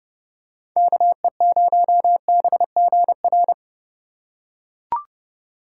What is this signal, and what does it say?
Morse code, heard as a keyed tone, about 28 wpm, 700 Hz. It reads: KE0BGR